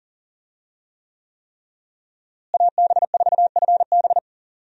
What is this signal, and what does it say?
Morse code, heard as a keyed tone, about 38 wpm, 700 Hz. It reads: AB4FB
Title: USA Call Signs - 5 Characters - Repeated 3x - 40wpm